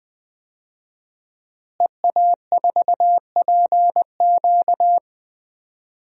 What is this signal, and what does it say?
Morse code, heard as a keyed tone, about 21 wpm, 700 Hz. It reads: EA4PQ
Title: DXCC-100 Call Signs - 5 Characters - Repeated 2x - 20wpm